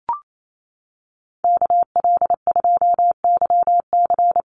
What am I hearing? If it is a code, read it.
Morse code, heard as a keyed tone, about 28 wpm, 700 Hz. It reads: KL2YC
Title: USA Call Signs Repeated 3x 28wpm